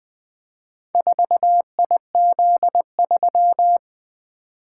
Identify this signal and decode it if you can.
Morse code, heard as a keyed tone, about 20 wpm, 700 Hz. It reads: 4IZ3
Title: Four Letter-Number Combinations - 20wpm